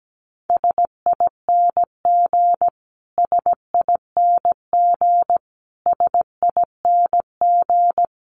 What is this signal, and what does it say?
Morse code, heard as a keyed tone, about 17 wpm, 700 Hz. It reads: SING SING SING
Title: Top 500 Words Repeated 17wpm